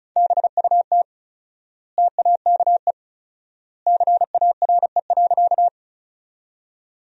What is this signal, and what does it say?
Morse code, heard as a keyed tone, about 34 wpm, 700 Hz. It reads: BUT TAKE CARE.
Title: 4x Word-Spacing - Sentences from Top 300 Words - 35wpm